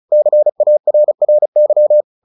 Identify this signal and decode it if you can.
Morse code, heard as a keyed tone, about 34 wpm, 600 Hz. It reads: CARRY
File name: the old man and the sea / the old man and the sea 1 35wpm